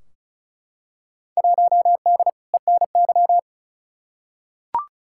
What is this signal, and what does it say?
Morse code, heard as a keyed tone, about 34 wpm, 700 Hz. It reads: 1D ENY
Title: Field Day Exchanges - 35wpm